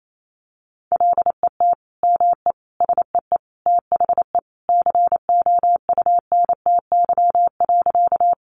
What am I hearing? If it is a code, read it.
Morse code, heard as a keyed tone, about 28 wpm, 700 Hz. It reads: LET ME SEE THE COUNTY.
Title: Sentences from Top 1200 Words 28wpm